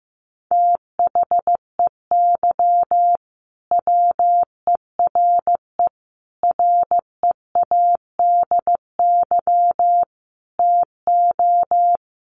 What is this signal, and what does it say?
Morse code, heard as a keyed tone, about 15 wpm, 700 Hz. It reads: THEY WERE READY TO
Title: A Sound Of Thunder 15wpm